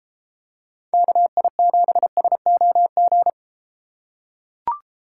Morse code, heard as a keyed tone, about 32 wpm, 700 Hz. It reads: KI7SOG